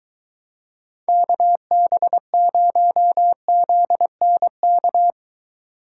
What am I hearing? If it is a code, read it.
Morse code, heard as a keyed tone, about 23 wpm, 700 Hz. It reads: KB0ZNK